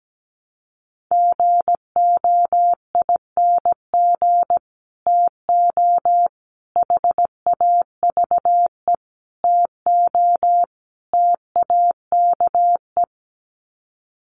Morse code, heard as a keyed tone, about 17 wpm, 700 Hz. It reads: GOING TO HAVE TO TAKE
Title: Sets of 5 Words Encore 17wpm